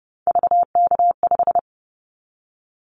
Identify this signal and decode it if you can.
Morse code, heard as a keyed tone, about 30 wpm, 700 Hz. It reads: VK5
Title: Callsign Prefixes for Top 100 DXCC - 30wpm